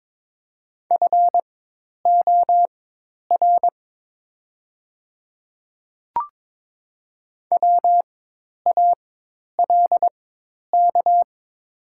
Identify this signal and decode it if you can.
Morse code, heard as a keyed tone, about 22 wpm, spaced as for 10 wpm, 700 Hz. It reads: FOR WALK
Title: 4x Character Spacing - Top 500 Words - 22wpm